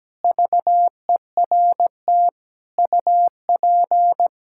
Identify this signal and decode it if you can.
Morse code, heard as a keyed tone, about 17 wpm, 700 Hz. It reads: VERT UP
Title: K1USN SST QSO Practice 17wpm